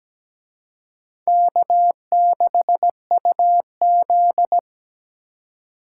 Morse code, heard as a keyed tone, about 20 wpm, 700 Hz. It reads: K6UZ
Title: Speed Racing - USA 1x2 Call Signs 17wpm